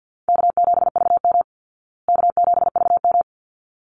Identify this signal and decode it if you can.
Morse code, heard as a keyed tone, about 50 wpm, 700 Hz. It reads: K7VM K7VM
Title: Easy Call Signs Repeated 2x - 50wpm